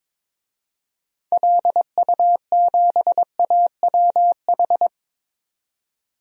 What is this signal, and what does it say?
Morse code, heard as a keyed tone, about 22 wpm, 700 Hz. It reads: LU7AWH
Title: DXCC-100 Call Signs - 6 Characters - 22wpm